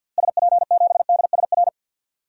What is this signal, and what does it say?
Morse code, heard as a keyed tone, about 51 wpm, 700 Hz. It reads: SP7DSR
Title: DXCC-100 Call Signs - 6 Characters - Repeated 3x - 50wpm